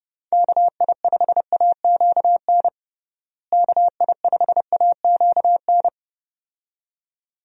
Morse code, heard as a keyed tone, about 30 wpm, 700 Hz. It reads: KI5AQN KI5AQN
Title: USA Call Signs - 6 Characters - Repeated 2x - 30wpm